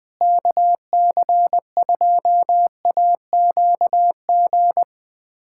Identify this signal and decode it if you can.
Morse code, heard as a keyed tone, about 20 wpm, 700 Hz. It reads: KC2AQG
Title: USA Callsign DE Callsign - 6 Characters - 20wpm